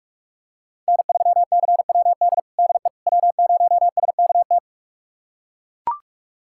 Morse code, heard as a keyed tone, about 43 wpm, 700 Hz. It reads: N3CWD DE W0SKT
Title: USA Callsign DE Callsign - 5 Characters - 45wpm